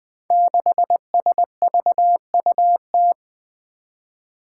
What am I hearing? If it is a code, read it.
Morse code, heard as a keyed tone, about 20 wpm, 700 Hz. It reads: 6SVUT
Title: Five Challenging Letter-Number Combinations 20wpm